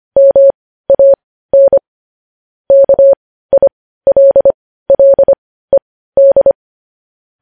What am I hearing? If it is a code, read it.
Morse code, heard as a keyed tone, about 25 wpm, spaced as for 15 wpm, 550 Hz. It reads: MAN KILLED
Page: news-headlines-in-morse-code-at-15-wpm